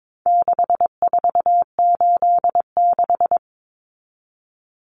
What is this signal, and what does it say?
Morse code, heard as a keyed tone, about 22 wpm, 700 Hz. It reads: 6486
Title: Numbers - 4-Digit 22wpm